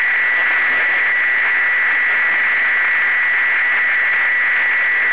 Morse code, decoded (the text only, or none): none